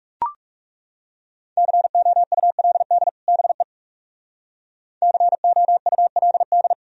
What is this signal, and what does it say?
Morse code, heard as a keyed tone, about 39 wpm, 700 Hz. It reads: COULD BE COULD